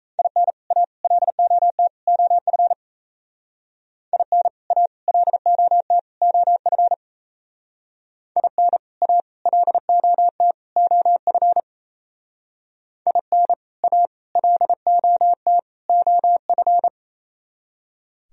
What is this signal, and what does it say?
Morse code, heard as a keyed tone, about 34 wpm, 700 Hz. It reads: IN A LOT OF IN A LOT OF IN A LOT OF IN A LOT OF
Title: Speed-Racing - Sets of 4 Words 28wpm